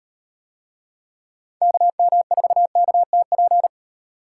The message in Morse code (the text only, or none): KM4KTP